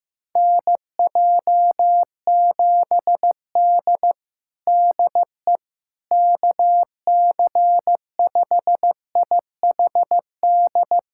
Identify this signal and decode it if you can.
Morse code, heard as a keyed tone, about 15 wpm, 700 Hz. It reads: NJ7D DE KC5IHD
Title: USA Callsign DE Callsign 15wpm